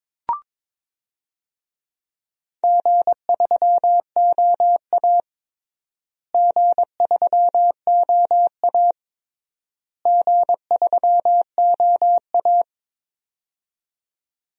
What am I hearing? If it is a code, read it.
Morse code, heard as a keyed tone, about 22 wpm, 700 Hz. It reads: G3OA G3OA G3OA
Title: DXCC-100 Call Signs - 4 Characters - Repeated 3x - 22wpm